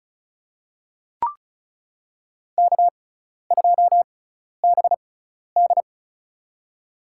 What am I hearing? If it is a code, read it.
Morse code, heard as a keyed tone, about 36 wpm, spaced as for 12 wpm, 700 Hz. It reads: K2BD
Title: USA Call Signs - 6x Character Spacing - 35wpm